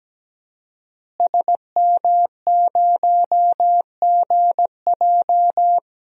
Morse code, heard as a keyed tone, about 17 wpm, 700 Hz. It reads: SM0GJ
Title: DXCC-100 Call Signs - 5 Characters - Repeated 3x - 17wpm